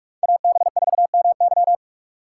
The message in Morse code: AB4MY

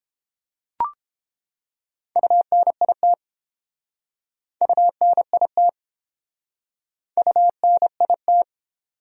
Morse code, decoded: UNIT UNIT UNIT

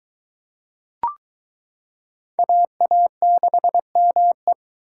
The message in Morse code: AA6ME

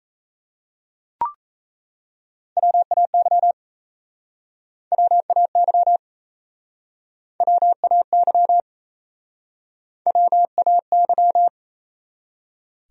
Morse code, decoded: WAY WAY WAY WAY

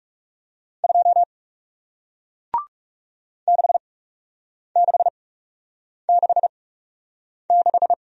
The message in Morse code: 2 6666

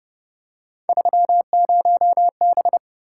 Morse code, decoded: 30B

